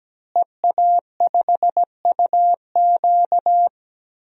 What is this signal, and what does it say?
Morse code, heard as a keyed tone, about 17 wpm, 700 Hz. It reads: EA5UQ
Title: DXCC-100 Call Signs - 5 Characters - Repeated 2x - 17wpm